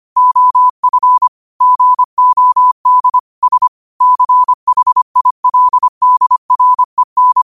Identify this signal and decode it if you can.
Morse code, heard as a keyed tone, about 25 wpm, 1000 Hz. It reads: OF GOD S CHILDREN
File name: qotdfiles / 25wpm